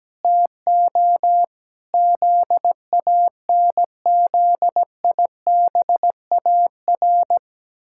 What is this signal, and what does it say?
Morse code, heard as a keyed tone, about 17 wpm, 700 Hz. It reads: TO ZANZIBAR
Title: Pangram Challenge v2 - 17wpm